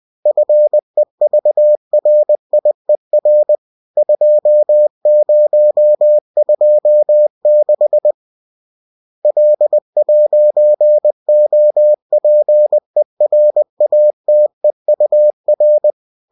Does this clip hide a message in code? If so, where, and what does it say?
Yes: Morse code from 0.3 to 15.9 s, heard as a keyed tone, about 20 wpm, 600 Hz. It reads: FEVRIER 2026 L'OPERATEUR